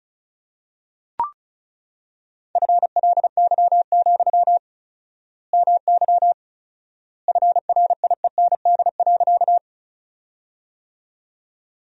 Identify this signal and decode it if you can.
Morse code, heard as a keyed tone, about 36 wpm, 700 Hz. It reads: FLY, MY FRIEND.
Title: 4x Word-Spacing - Sentences from Top 500 Words - 35wpm